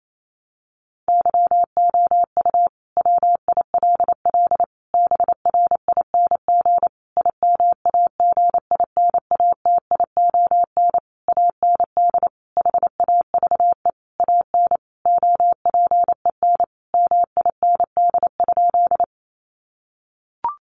Morse code, heard as a keyed tone, about 28 wpm, 700 Hz. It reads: YOU WILL BRING IMAGINATION AND HAVE AN OPEN MIND?
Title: Sentences from Top 1300 Words 28wpm